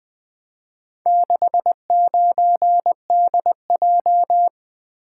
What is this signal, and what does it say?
Morse code, heard as a keyed tone, about 20 wpm, 700 Hz. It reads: 69DJ
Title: Four Letter-Number Combinations - 20wpm